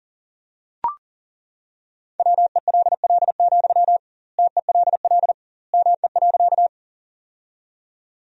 Morse code, heard as a keyed tone, about 40 wpm, 700 Hz. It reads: WELL, TELL ME.
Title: 2x Word-Spacing - Sentences from Top 200 Words - 40wpm